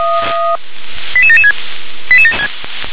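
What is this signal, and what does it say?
Five-tone selective call (ZVEI), heard as quick runs of five tones, tones of about 70 ms. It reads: 88776
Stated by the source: ZVEI selective call received on an SDR.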